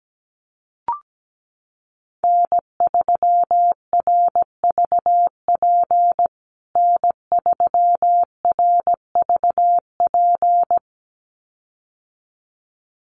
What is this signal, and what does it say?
Morse code, heard as a keyed tone, about 17 wpm, 700 Hz. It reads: N3RVP N3RVP
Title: USA Call Signs Repeated 2x 17wpm